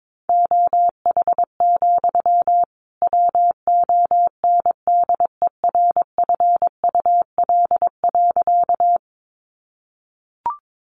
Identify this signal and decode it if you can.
Morse code, heard as a keyed tone, about 22 wpm, 700 Hz. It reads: OH, WONDERFUL.
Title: Sentences from Top 1100 Words 22wpm